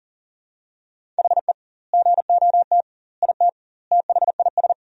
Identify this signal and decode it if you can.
Morse code, heard as a keyed tone, about 40 wpm, 700 Hz. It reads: HE GOT IT THIS
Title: 2x Word-Spacing - Sentences from Top 400 Words - 40wpm